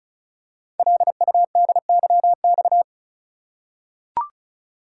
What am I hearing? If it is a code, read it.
Morse code, heard as a keyed tone, about 34 wpm, 690 Hz. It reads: LUDYX
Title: Code Groups - 5 Letters - 35wpm